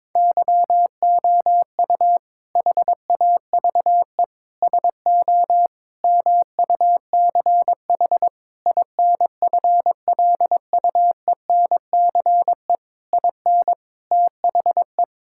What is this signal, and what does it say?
Morse code, heard as a keyed tone, about 21 wpm, 700 Hz. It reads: YOU HAVE SO MUCH INFLUENCE IN THE
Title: Sentences from Top 1600 Words 22wpm